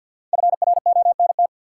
Morse code, heard as a keyed tone, about 53 wpm, 700 Hz. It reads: FRONT
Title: Sets of 2 Words - New Word Drill - 50wpm